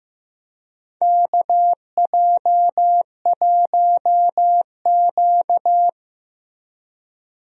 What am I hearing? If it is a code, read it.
Morse code, heard as a keyed tone, about 15 wpm, 700 Hz. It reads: KJ1Q
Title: USA Call Signs - 4 Characters - Repeated 2x - 15wpm